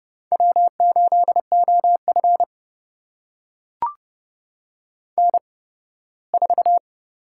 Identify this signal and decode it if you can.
Morse code, heard as a keyed tone, about 30 wpm, spaced as for 18 wpm, 700 Hz. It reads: W8OF N4
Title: USA Call Signs - 4 Characters - 8x Character Spacing - 30wpm